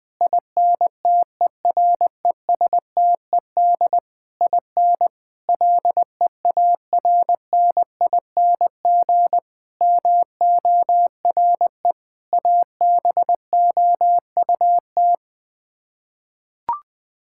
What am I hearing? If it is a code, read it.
Morse code, heard as a keyed tone, about 20 wpm, 700 Hz. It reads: INTERESTED IN LEARNING MORE ABOUT
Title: Sets of 8 Words - 20wpm